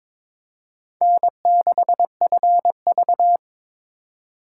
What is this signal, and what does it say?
Morse code, heard as a keyed tone, about 26 wpm, 700 Hz. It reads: N6FV
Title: Speed Racing - USA 1x2 Call Signs 22wpm